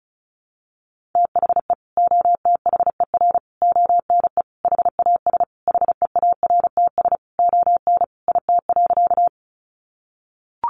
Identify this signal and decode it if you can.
Morse code, heard as a keyed tone, about 34 wpm, 700 Hz. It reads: THE OTHER ONE HAS HEARTS ON IT.
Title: Additional Easy Phrases 35wpm